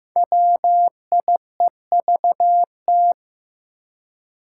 Morse code, heard as a keyed tone, about 15 wpm, 700 Hz. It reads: WIEVT
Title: Five Challenging Letter-Number Combinations 15wpm